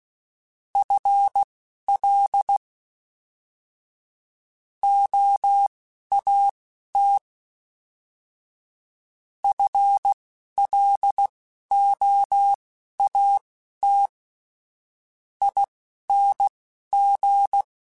Morse code, heard as a keyed tone, about 16 wpm, spaced as for 12 wpm, 800 Hz. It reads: FL OAT FLOAT ING
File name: FL / FL mp3s / Fast FL @10